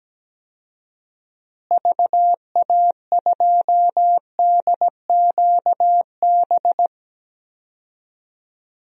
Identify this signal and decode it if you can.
Morse code, heard as a keyed tone, about 17 wpm, 700 Hz. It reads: VA2DQB
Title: DXCC-100 Call Signs - 6 Characters - Repeated 3x - 17wpm